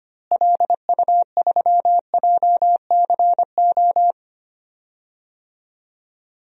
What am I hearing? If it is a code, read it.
Morse code, heard as a keyed tone, about 25 wpm, 700 Hz. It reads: LU3JCO
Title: DXCC-100 Call Signs - 6 Characters - Repeated 3x - 25wpm